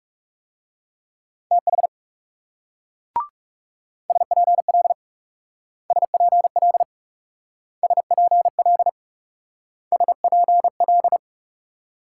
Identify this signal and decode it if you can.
Morse code, heard as a keyed tone, about 37 wpm, 700 Hz. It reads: TH SPL SPL SPL SPL
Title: Speed-Racing Multi-letter Phonemes - 30wpm